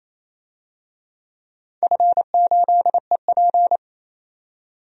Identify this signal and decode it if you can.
Morse code, heard as a keyed tone, about 27 wpm, 690 Hz. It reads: F8EP
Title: DXCC-100 Call Signs - 4 Characters - 28wpm